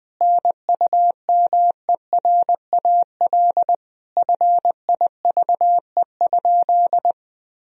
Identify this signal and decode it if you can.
Morse code, heard as a keyed tone, about 20 wpm, 700 Hz. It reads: NUMERAL FIVE?
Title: Sentences from Top 1000 Words - Review Part 2 - 20wpm